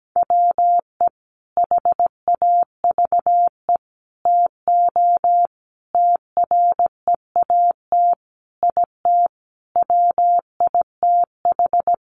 Morse code, heard as a keyed tone, about 17 wpm, 700 Hz. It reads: WE HAVE TO TREAT IT WITH
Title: Sentences from Top 1300 Words 17wpm